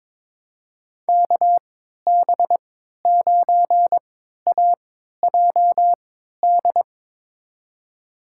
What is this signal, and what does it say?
Morse code, heard as a keyed tone, about 21 wpm, spaced as for 12 wpm, 700 Hz. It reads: KB9AJD
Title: USA Call Signs - 3x Character Spacing - 22wpm